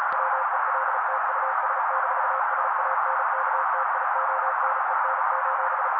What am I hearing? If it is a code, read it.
Morse code, heard as a keyed tone, about 36 wpm, 530 Hz. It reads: WELT US-SANKTIONEN:
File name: daily / 35wpm noise